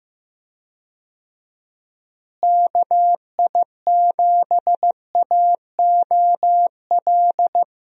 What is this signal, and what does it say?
Morse code, heard as a keyed tone, about 15 wpm, 700 Hz. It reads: KI7AOL
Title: USA Call Signs - 6 Characters - Repeated 3x - 15wpm